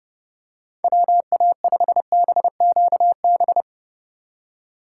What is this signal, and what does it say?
Morse code, heard as a keyed tone, about 30 wpm, 700 Hz. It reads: WA5BQB
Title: USA Call Signs - 4x Character Spacing - 30wpm